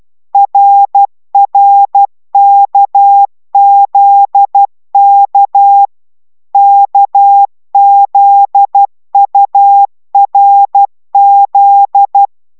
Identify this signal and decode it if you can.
Morse code, heard as a keyed tone, about 12 wpm, 800 Hz. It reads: RRKZK KZURZ
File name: taviro tanulas / cw 69 15 10